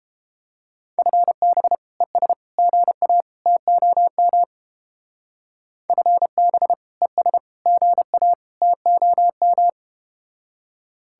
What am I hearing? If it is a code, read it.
Morse code, heard as a keyed tone, about 32 wpm, 700 Hz. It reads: FB ES GA TOM FB ES GA TOM